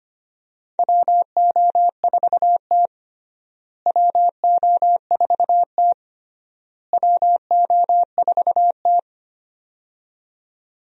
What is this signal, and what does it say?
Morse code, heard as a keyed tone, about 25 wpm, 700 Hz. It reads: WO4T WO4T WO4T
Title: USA Call Signs - 4 Characters - Repeated 3x - 25wpm